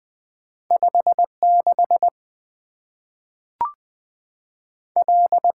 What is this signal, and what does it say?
Morse code, heard as a keyed tone, about 20 wpm, 700 Hz. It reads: 5 6 L
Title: Two Letter-Number Combinations - 20wpm